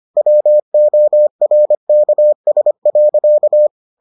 Morse code, heard as a keyed tone, about 25 wpm, 600 Hz. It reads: WORKS.